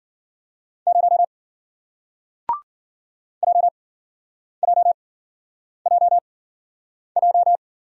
Morse code, heard as a keyed tone, about 55 wpm, spaced as for 9 wpm, 700 Hz. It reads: 0 JJJJ